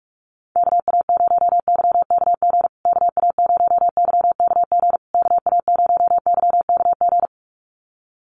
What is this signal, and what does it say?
Morse code, heard as a keyed tone, about 44 wpm, 700 Hz. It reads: KA0YKG KA0YKG KA0YKG
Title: USA Call Signs Repeated 3x 45wpm